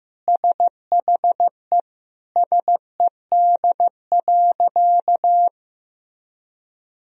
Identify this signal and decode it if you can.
Morse code, heard as a keyed tone, about 15 wpm, 700 Hz. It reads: SHE SED.
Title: Sentences from Top 500 Words and Abbreviations 15wpm